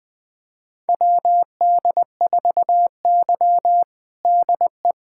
Morse code, heard as a keyed tone, about 21 wpm, 700 Hz. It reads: WD4Y DE
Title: USA Callsign DE Callsign - 4 Characters - 20wpm